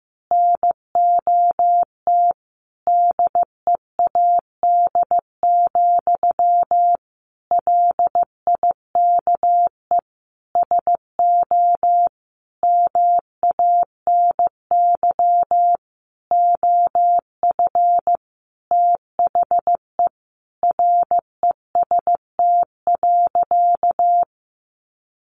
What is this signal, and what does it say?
Morse code, heard as a keyed tone, about 15 wpm, 700 Hz. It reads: NOT DEAD, LIKE SO MANY OF THE REST.
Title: Sentences from Top 1500 Words 15wpm